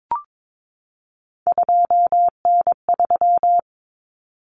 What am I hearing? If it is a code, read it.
Morse code, heard as a keyed tone, about 22 wpm, 700 Hz. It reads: 2N3
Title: Common Cut Numbers - 3-Digit 22wpm